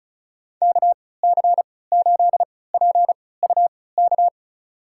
Morse code, heard as a keyed tone, about 36 wpm, spaced as for 20 wpm, 700 Hz. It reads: KC8PUK
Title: USA Call Signs - 6 Characters - 3x Character Spacing - 35wpm